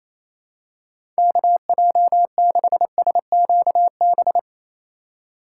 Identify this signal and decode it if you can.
Morse code, heard as a keyed tone, about 28 wpm, 700 Hz. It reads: KJ6SQB
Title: USA Call Signs - 6 Characters - 3x Character Spacing - 28wpm